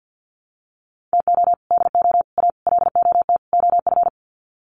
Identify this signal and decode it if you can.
Morse code, heard as a keyed tone, about 50 wpm, 700 Hz. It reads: TO DO A LOT OF
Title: Sets of 5 Words Encore 50wpm